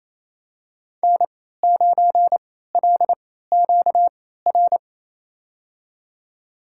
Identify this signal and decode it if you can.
Morse code, heard as a keyed tone, about 28 wpm, spaced as for 16 wpm, 700 Hz. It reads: N9LQR
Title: USA Call Signs - 3x Character Spacing - 28wpm